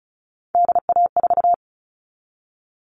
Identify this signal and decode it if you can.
Morse code, heard as a keyed tone, about 34 wpm, 700 Hz. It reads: DA4
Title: All Cut Numbers - 3-Digit 35wpm